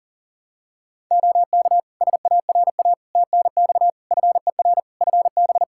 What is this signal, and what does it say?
Morse code, heard as a keyed tone, about 40 wpm, 700 Hz. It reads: OK SARA TNX FER FB
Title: Practice QSO - Exchange 4 - Ending - 40wpm